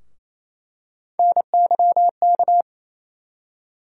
Morse code, heard as a keyed tone, about 28 wpm, 700 Hz. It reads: NYK
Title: USA Call Sign Postfixes - 3 Letters 28wpm